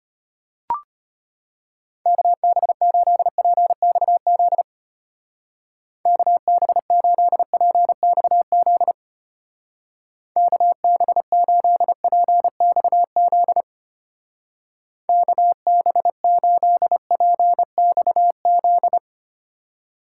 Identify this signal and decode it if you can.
Morse code, heard as a keyed tone, about 30 wpm, 700 Hz. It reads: KB8PXZ KB8PXZ KB8PXZ KB8PXZ
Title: Speed Racing - USA 2x3 Call Signs 25wpm